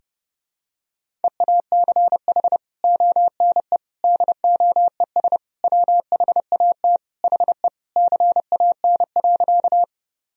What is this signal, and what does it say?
Morse code, heard as a keyed tone, about 30 wpm, 700 Hz. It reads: EACH ONE DOES WHAT HE CAN.
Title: Sentences from Top 1000 Words - Review Part 1 - 30wpm